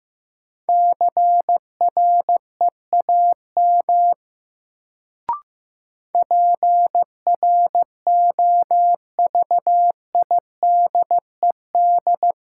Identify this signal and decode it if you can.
Morse code, heard as a keyed tone, about 15 wpm, 700 Hz. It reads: CREAM PROVIDED
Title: Sentences from Top 1500 Words - New Word Drill - 15wpm